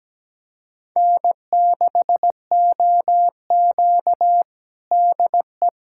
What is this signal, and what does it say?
Morse code, heard as a keyed tone, about 17 wpm, 700 Hz. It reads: N6OQ DE
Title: USA Callsign DE Callsign - 4 Characters - 17wpm